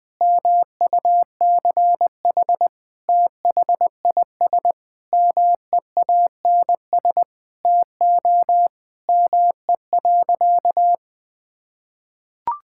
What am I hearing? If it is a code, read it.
Morse code, heard as a keyed tone, about 21 wpm, 700 Hz. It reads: MUCH THIS MEANS TO ME.
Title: Sentences with 5-Word Phrases - 20wpm